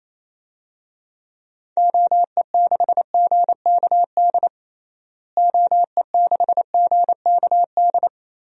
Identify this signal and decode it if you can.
Morse code, heard as a keyed tone, about 28 wpm, 700 Hz. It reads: OE6GKD OE6GKD
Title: DXCC-100 Call Signs - 6 Characters - Repeated 2x - 28wpm